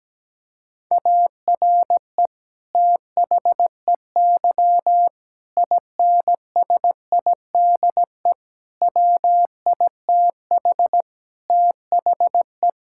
Morse code, heard as a keyed tone, about 17 wpm, 700 Hz. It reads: ARE THEY INSIDE WITH THE
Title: Sentences from Top 1000 Words - Review Part 6 - 17wpm